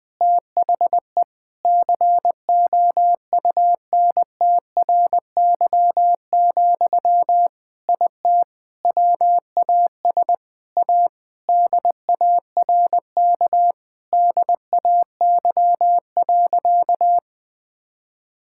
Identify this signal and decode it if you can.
Morse code, heard as a keyed tone, about 20 wpm, 700 Hz. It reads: THE COUNTRY, IT WAS A DARK DAY.
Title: Sentences from Top 1000 Words - Review Part 5 - 20wpm